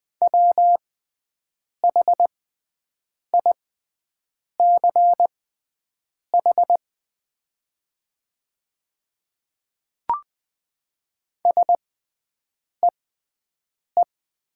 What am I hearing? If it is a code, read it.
Morse code, heard as a keyed tone, about 20 wpm, spaced as for 7 wpm, 700 Hz. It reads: WHICH SEE
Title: 6x Character Spacing - Top 100 Words - 20wpm